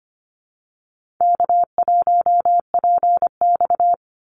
Morse code, heard as a keyed tone, about 25 wpm, 700 Hz. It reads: K1PX
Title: Easy Call Signs - 25wpm